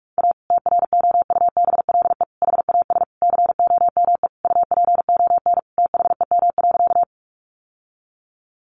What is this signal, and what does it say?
Morse code, heard as a keyed tone, about 44 wpm, 700 Hz. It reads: A TROUBLE HAS COME UPON THEM.
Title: Sentences from Top 1200 Words 45wpm